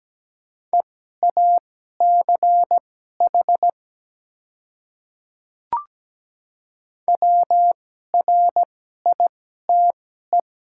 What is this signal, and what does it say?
Morse code, heard as a keyed tone, about 17 wpm, spaced as for 12 wpm, 700 Hz. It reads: EACH WRITE
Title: 2x Character Spacing - Top 100 Words - 17wpm